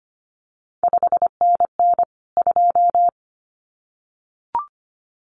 Morse code, heard as a keyed tone, about 25 wpm, 700 Hz. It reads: 5NN 2
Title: CQ WW DX Contest Exchange Encore - 25wpm